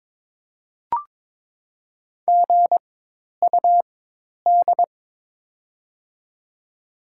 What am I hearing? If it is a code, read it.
Morse code, heard as a keyed tone, about 22 wpm, spaced as for 10 wpm, 700 Hz. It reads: GUD
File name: Top 20 QSO Elements - 4x Character Spacing - 22wpm